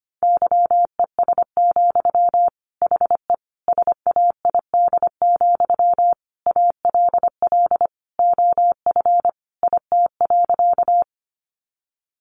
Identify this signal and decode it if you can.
Morse code, heard as a keyed tone, about 25 wpm, 700 Hz. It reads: YES, HE SAID, ALL OF IT.
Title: Sentences from Top 500 Words 25wpm